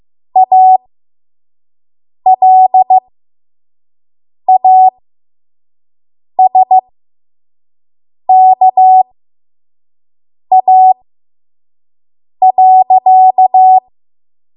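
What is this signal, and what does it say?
Morse code, heard as a keyed tone, about 15 wpm, spaced as for 5 wpm, 750 Hz. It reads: ALASKA.
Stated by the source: 5 WPM Morse code practice.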